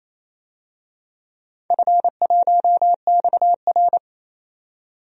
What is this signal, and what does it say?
Morse code, heard as a keyed tone, about 28 wpm, 700 Hz. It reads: F1XR
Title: DXCC-100 Call Signs - 4 Characters - 28wpm